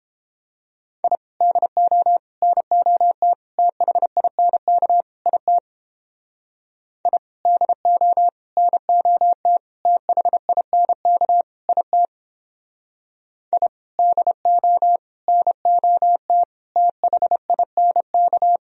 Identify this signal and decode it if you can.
Morse code, heard as a keyed tone, about 29 wpm, 700 Hz. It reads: I DO NOT THINK IT I DO NOT THINK IT I DO NOT THINK